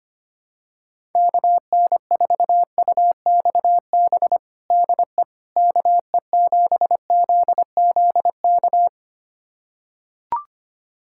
Morse code, heard as a keyed tone, about 25 wpm, 700 Hz. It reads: KN4UXB DE KE7ZZK
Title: USA Callsign DE Callsign - 6 Characters - 25wpm